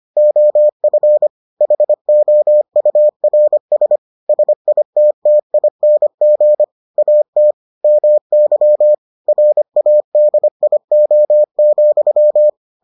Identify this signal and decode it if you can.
Morse code, heard as a keyed tone, about 25 wpm, 600 Hz. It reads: OF HOURS SITTING AT MY RADIO,